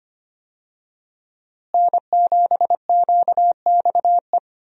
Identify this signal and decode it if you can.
Morse code, heard as a keyed tone, about 25 wpm, 700 Hz. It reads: N7QXE
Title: USA Call Signs - 5 Characters - Repeated 2x - 25wpm